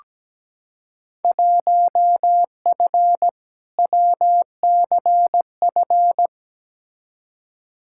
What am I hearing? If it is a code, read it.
Morse code, heard as a keyed tone, about 17 wpm, 700 Hz. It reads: 1F WCF
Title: Field Day Exchanges - 17wpm